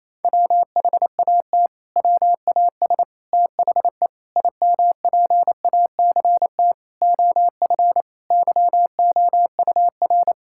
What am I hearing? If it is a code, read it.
Morse code, heard as a keyed tone, about 28 wpm, 700 Hz. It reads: WHAT WAS THE IMPACT OF YOUR
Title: Sentences from Top 1000 Words - Review Part 6 - 28wpm